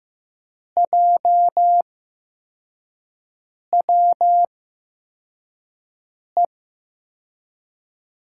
Morse code, heard as a keyed tone, about 15 wpm, spaced as for 4 wpm, 700 Hz. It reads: JWE
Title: USA Call Signs - 6 Characters - 8x Character Spacing - 15wpm